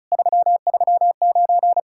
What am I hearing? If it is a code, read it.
Morse code, heard as a keyed tone, about 43 wpm, 700 Hz. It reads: 339
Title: Speed-Racing Common QSO Phrases - 35wpm